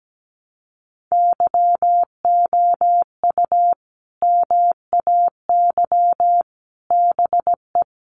Morse code, heard as a keyed tone, about 17 wpm, 700 Hz. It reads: YOU MAY BE
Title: Sets of 5 Words Encore 17wpm